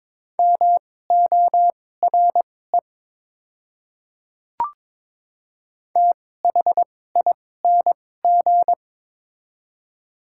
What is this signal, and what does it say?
Morse code, heard as a keyed tone, about 23 wpm, spaced as for 16 wpm, 700 Hz. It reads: MORE THING
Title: 2x Character Spacing - Top 200 Words - 22wpm